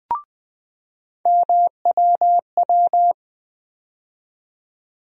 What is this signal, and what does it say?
Morse code, heard as a keyed tone, about 20 wpm, 700 Hz. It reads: MWW
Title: USA Call Sign Postfixes - 3 Letters 20wpm